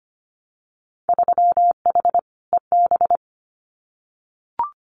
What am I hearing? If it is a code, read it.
Morse code, heard as a keyed tone, about 25 wpm, 700 Hz. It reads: 3H EB
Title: Winter Field Day Exchanges Encore 25wpm